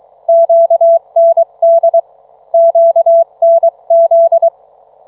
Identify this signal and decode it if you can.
Morse code, heard as a keyed tone, about 23 wpm, 660 Hz, with radio noise. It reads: QND QNZ